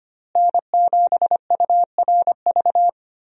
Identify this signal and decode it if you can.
Morse code, heard as a keyed tone, about 25 wpm, 700 Hz. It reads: N7URV
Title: USA Call Signs - 5 Characters - Repeated 3x - 25wpm